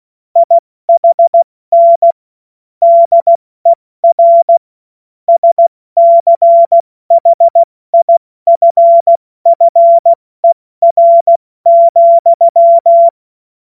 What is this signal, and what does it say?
Morse code, heard as a keyed tone, about 16 wpm, 680 Hz. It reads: IHN DER SCHIFFER,
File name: Morsekurs / Witze / Witz 21. Februar 2023 70 F80